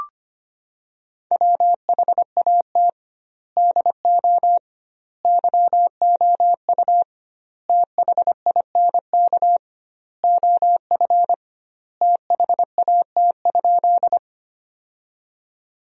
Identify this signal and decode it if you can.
Morse code, heard as a keyed tone, about 25 wpm, 700 Hz. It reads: WHAT DO YOU THINK OF THAT?
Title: 2x Word-Spacing - Sentences from Top 200 Words - 25wpm